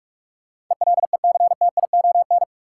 Morse code, heard as a keyed tone, about 45 wpm, 700 Hz. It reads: ELECTION
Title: Sentences from Top 1500 Words - New Word Drill Alt - 45wpm